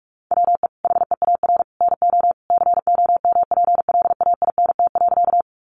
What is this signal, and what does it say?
Morse code, heard as a keyed tone, about 46 wpm, 700 Hz. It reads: WE HEAR NO COMPLAINT.
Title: Sentences from Top 1400 Words 45wpm